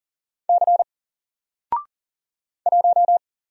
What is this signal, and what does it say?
Morse code, heard as a keyed tone, about 40 wpm, 700 Hz. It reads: C 1
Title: Single Character (Letter:Number:Punctuation:Prosign) Alt 40wpm